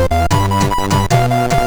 There is no code.